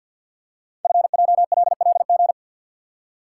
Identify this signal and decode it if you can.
Morse code, heard as a keyed tone, about 48 wpm, 690 Hz. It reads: FJLLG
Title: Code Groups - 5 Letters - 50wpm